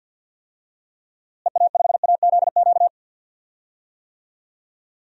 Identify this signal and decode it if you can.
Morse code, heard as a keyed tone, about 48 wpm, 700 Hz. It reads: EA5AZQ
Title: DXCC-100 Call Signs - 6 Characters - Repeated 3x - 50wpm